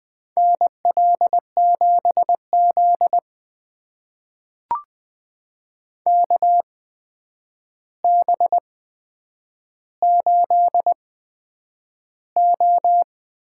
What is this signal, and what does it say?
Morse code, heard as a keyed tone, about 20 wpm, spaced as for 9 wpm, 700 Hz. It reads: NL7Z KB8O
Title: USA Call Signs - 8x Character Spacing - 20wpm